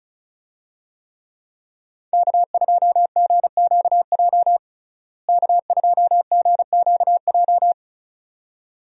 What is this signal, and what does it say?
Morse code, heard as a keyed tone, about 34 wpm, 700 Hz. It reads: K2GQJ K2GQJ
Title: USA Call Signs - 5 Characters - Repeated 2x - 35wpm